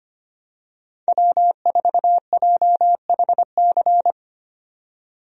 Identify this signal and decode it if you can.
Morse code, heard as a keyed tone, about 25 wpm, 700 Hz. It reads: W4JHC
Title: USA Call Signs - 5 Characters - 25wpm